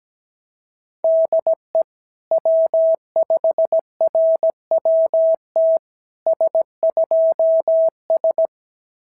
Morse code, heard as a keyed tone, about 17 wpm, 690 Hz. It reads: DE W5RWT S2S
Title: SOTA QSO Practice 17wpm